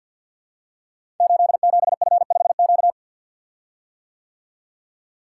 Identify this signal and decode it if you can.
Morse code, heard as a keyed tone, about 53 wpm, 700 Hz. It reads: 87F5Q
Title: Code Groups - 5 Letter:Numbers - 50wpm